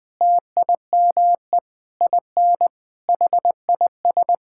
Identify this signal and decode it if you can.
Morse code, heard as a keyed tone, about 21 wpm, 700 Hz. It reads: TIME IN HIS
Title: Sentences with 7-Word Phrases - 20wpm